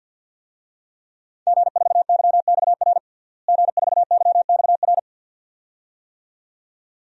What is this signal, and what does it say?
Morse code, heard as a keyed tone, about 52 wpm, 700 Hz. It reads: G4YXR G4YXR
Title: DXCC-100 Call Signs - 5 Characters - Repeated 2x - 50wpm